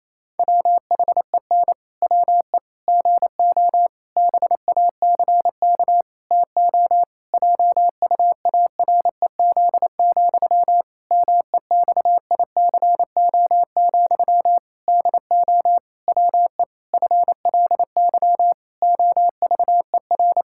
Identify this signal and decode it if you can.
Morse code, heard as a keyed tone, about 28 wpm, 700 Hz. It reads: WHEN WE GO BACK TO JUAREZ, MEXICO, DO WE FLY OVER
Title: Pangram Challenge v2 - 28wpm